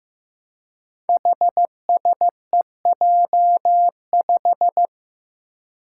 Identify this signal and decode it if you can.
Morse code, heard as a keyed tone, about 15 wpm, 700 Hz. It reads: HSEJ5
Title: Five Challenging Letter-Number Combinations 15wpm